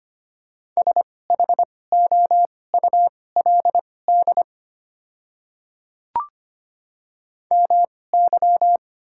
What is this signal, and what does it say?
Morse code, heard as a keyed tone, about 25 wpm, spaced as for 18 wpm, 700 Hz. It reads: SHOULD MY
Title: 2x Character Spacing - Top 300 Words - 25wpm